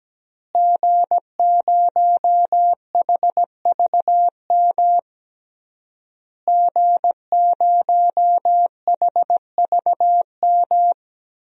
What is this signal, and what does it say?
Morse code, heard as a keyed tone, about 17 wpm, 700 Hz. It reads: G0HVM G0HVM
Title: DXCC-100 Call Signs - 5 Characters - Repeated 3x - 17wpm